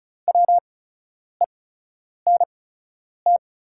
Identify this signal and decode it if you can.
Morse code, heard as a keyed tone, about 35 wpm, spaced as for 10 wpm, 690 Hz. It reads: WENT
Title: 8x Character Spacing - Top 300 Words - 35wpm